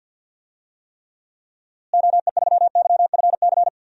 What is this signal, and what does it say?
Morse code, heard as a keyed tone, about 53 wpm, 700 Hz. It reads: OE3YFC
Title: DXCC-100 Call Signs - 6 Characters - Repeated 2x - 50wpm